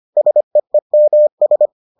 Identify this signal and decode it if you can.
Morse code, heard as a keyed tone, about 25 wpm, 600 Hz. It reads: SEEMS